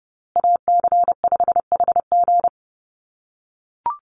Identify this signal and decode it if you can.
Morse code, heard as a keyed tone, about 30 wpm, 700 Hz. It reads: AC5HG
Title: USA Call Signs Repeated 2x 30wpm